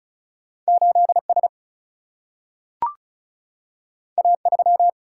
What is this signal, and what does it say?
Morse code, heard as a keyed tone, about 36 wpm, 700 Hz. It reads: 8S A3
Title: Single-Letter Single-Number 35wpm